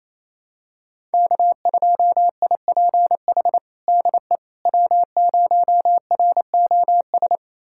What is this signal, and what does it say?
Morse code, heard as a keyed tone, about 28 wpm, 700 Hz. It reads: K2IPH DE W0ROS
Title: USA Callsign DE Callsign - 5 Characters - 28wpm